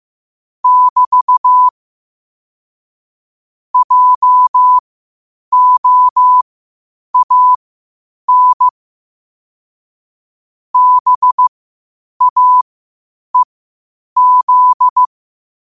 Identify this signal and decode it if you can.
Morse code, heard as a keyed tone, about 15 wpm, spaced as for 8 wpm, 1000 Hz. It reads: <BT> JOAN BAEZ
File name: qotdfiles / 8wpm